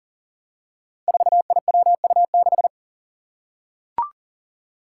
Morse code, heard as a keyed tone, about 39 wpm, 700 Hz. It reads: 4IWU6
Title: Five Challenging Letter-Number Combinations 40wpm